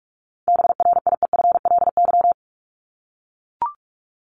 Morse code, heard as a keyed tone, about 44 wpm, 700 Hz. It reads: BRIEFLY
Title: LY Suffix - Top 100 Words with Base Words 45wpm